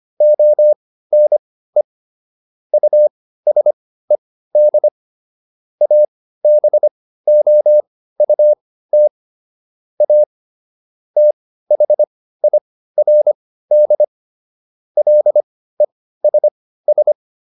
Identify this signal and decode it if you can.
Morse code, heard as a keyed tone, about 25 wpm, spaced as for 15 wpm, 600 Hz. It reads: ONE USED ABOUT A THIRD LESS